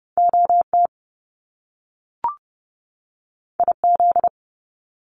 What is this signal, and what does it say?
Morse code, heard as a keyed tone, about 30 wpm, 700 Hz. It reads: OT IZ
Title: Top Two Letter Combinations in Words 30wpm